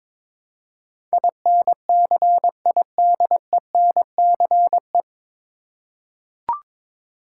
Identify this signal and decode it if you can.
Morse code, heard as a keyed tone, about 22 wpm, 700 Hz. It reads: INCIDENCE
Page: Morse Code Ninja practice 22wpm